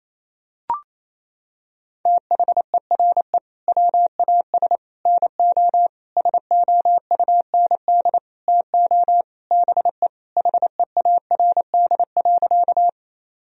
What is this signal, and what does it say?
Morse code, heard as a keyed tone, about 28 wpm, 700 Hz. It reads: THERE WAS NO SOUND TO BE HEARD.
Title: Sentences from Top 1000 Words - Review Part 1 - 28wpm